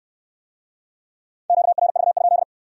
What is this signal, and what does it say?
Morse code, heard as a keyed tone, about 68 wpm, 700 Hz. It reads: QRV?